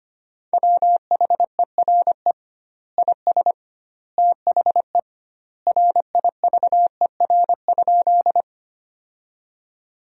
Morse code, heard as a keyed tone, about 25 wpm, 700 Hz. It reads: WHERE IS THE RIVER?
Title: 2x Word-Spacing - Sentences from Top 300 Words - 25wpm